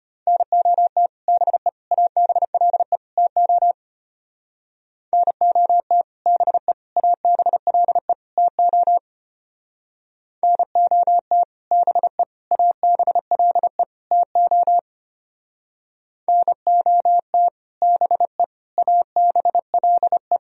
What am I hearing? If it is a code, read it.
Morse code, heard as a keyed tone, about 32 wpm, 700 Hz. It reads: NOT BE ABLE TO NOT BE ABLE TO NOT BE ABLE TO NOT BE ABLE